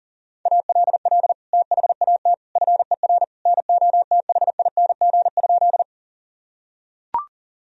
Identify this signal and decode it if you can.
Morse code, heard as a keyed tone, about 40 wpm, 700 Hz. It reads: ALL THAT FER NOTHING?
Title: Sentences from Top 500 Words and Abbreviations 40wpm